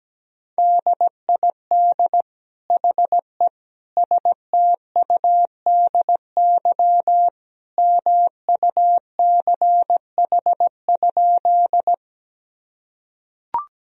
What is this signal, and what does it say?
Morse code, heard as a keyed tone, about 17 wpm, 700 Hz. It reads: DID HE STUDY MUCH?
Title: Sentences from Top 1000 Words - Review Part 1 - 17wpm